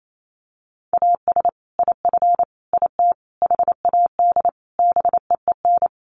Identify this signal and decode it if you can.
Morse code, heard as a keyed tone, about 28 wpm, 700 Hz. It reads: AS IF IT HAD BEEN
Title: Sets of 5 Words Encore2 28wpm